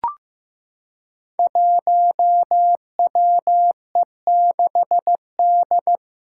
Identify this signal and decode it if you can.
Morse code, heard as a keyed tone, about 15 wpm, 700 Hz. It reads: 1WE6D
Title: Five Challenging Letter-Number Combinations 15wpm